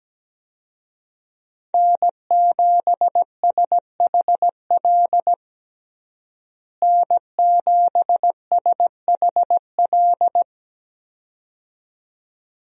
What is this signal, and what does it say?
Morse code, heard as a keyed tone, about 17 wpm, 700 Hz. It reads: N7SHL N7SHL
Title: USA Call Signs - 5 Characters - Repeated 2x - 17wpm